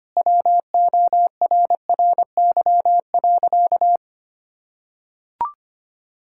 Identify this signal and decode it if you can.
Morse code, heard as a keyed tone, about 25 wpm, 700 Hz. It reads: WORRY.
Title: Pangram Challenge v2 - 25wpm